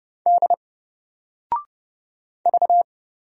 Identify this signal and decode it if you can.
Morse code, heard as a keyed tone, about 30 wpm, 700 Hz. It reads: D V
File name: Single-Letter-Number-30wpm